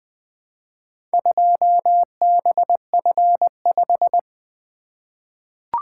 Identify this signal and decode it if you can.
Morse code, heard as a keyed tone, about 20 wpm, 700 Hz. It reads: 2BF5
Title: Four Letter-Number Combinations - 20wpm